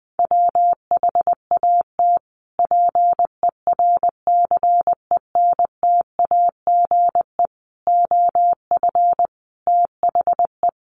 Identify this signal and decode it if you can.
Morse code, heard as a keyed tone, about 20 wpm, 700 Hz. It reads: WHAT PERCENTAGE OF THE
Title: Sentences from Top 1300 Words 20wpm